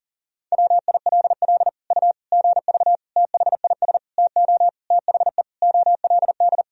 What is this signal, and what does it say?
Morse code, heard as a keyed tone, about 40 wpm, 700 Hz. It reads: WILL U GV THIS TO THE OLD
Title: Sentences from Top 200 Words and Abbreviations 40wpm